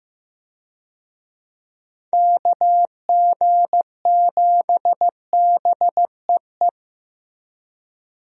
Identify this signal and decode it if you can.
Morse code, heard as a keyed tone, about 15 wpm, 700 Hz. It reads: KG7BEE
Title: USA Call Signs - 6 Characters - Repeated 3x - 15wpm